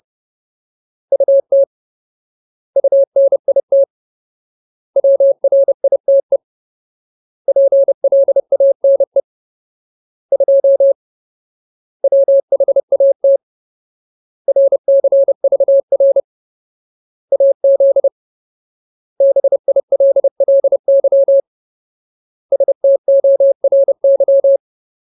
Morse code, heard as a keyed tone, about 30 wpm, 550 Hz. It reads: UT UNIT WRITE PLANE 2 WHAT RCVR AZ BILLY STORY
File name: Ultimate Word List / 30WPM Ultimate